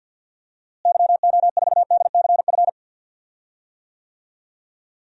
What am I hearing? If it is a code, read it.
Morse code, heard as a keyed tone, about 50 wpm, 700 Hz. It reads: YO4DCF
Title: DXCC-100 Call Signs - 6 Characters - Repeated 2x - 50wpm